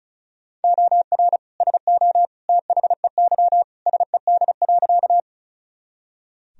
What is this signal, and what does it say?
Morse code, heard as a keyed tone, about 36 wpm, 700 Hz. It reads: OR SO THEY SED.
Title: Sentences from Top 100 Words and Abbreviations 35wpm